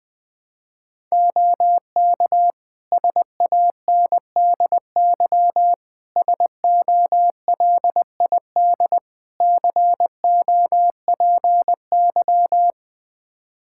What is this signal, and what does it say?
Morse code, heard as a keyed tone, about 20 wpm, 700 Hz. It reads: OK SANDY SOLID COPY
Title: Practice QSO - Exchange 3 - More Info - 20wpm